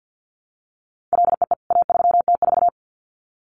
Morse code, heard as a keyed tone, about 50 wpm, 700 Hz. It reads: LEE A3T4
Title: CWT Exchange 50wpm